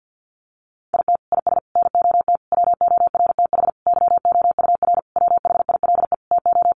Morse code, heard as a keyed tone, about 50 wpm, 700 Hz. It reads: IT IS NOT WORTH YOUR WHILE TO
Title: Sentences from Top 800 Words 50wpm